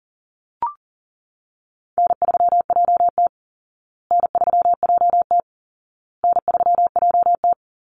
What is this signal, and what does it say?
Morse code, heard as a keyed tone, about 40 wpm, 700 Hz. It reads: N3JT N3JT N3JT
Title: Easy Call Signs Repeated 3x - 40wpm